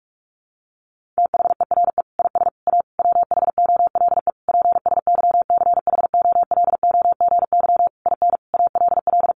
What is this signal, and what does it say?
Morse code, heard as a keyed tone, about 44 wpm, 700 Hz. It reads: THERE IS A WHOLE PSYCHOLOGY IN ALL
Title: Sentences from Top 1400 Words 45wpm